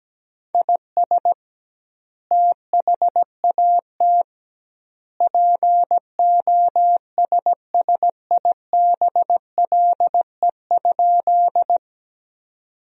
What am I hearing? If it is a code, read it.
Morse code, heard as a keyed tone, about 17 wpm, 700 Hz. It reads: IS THAT POSSIBLE?
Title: 2x Word-Spacing - Sentences from Top 500 Words - 17wpm